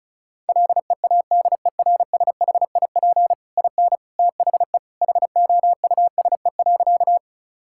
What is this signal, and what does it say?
Morse code, heard as a keyed tone, about 36 wpm, 700 Hz. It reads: LEADERSHIP IN THE HOUSE.
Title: Sentences from Top 1600 Words 35wpm